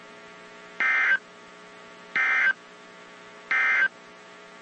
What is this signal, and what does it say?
Emergency Alert System (SAME) tones, heard as harsh buzzing bursts. It is heard from 0.7 to 4.0 s.
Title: WZ2523 - NOAA Weather Radio